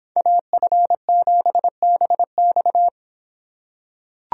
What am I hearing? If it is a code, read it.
Morse code, heard as a keyed tone, about 24 wpm, 700 Hz. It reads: AF7BX